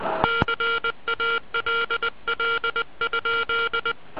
Morse code, heard as a keyed tone, about 20 wpm, 1390 Hz, with noise. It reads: CALL?